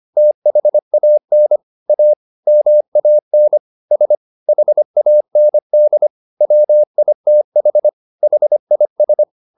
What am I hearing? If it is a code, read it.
Morse code, heard as a keyed tone, about 25 wpm, 600 Hz. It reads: THAN A MAN S HAND WITH HIS